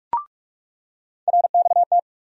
Morse code, heard as a keyed tone, about 42 wpm, 700 Hz. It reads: RXT
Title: USA Call Sign Postfixes - 3 Letters 45wpm